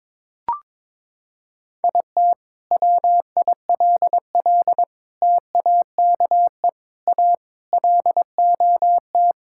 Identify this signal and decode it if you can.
Morse code, heard as a keyed tone, about 22 wpm, 700 Hz. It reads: IT WILL TAKE A LOT
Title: Sentences from Top 1000 Words - Review Part 6 - 22wpm